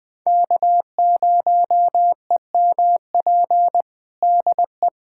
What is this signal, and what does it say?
Morse code, heard as a keyed tone, about 20 wpm, 700 Hz. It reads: K0EMP DE
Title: USA Callsign DE Callsign - 5 Characters - 20wpm